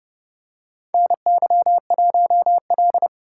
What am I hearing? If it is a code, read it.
Morse code, heard as a keyed tone, about 30 wpm, 700 Hz. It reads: NY1L
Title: USA Call Signs - 4 Characters - 6x Character Spacing - 30wpm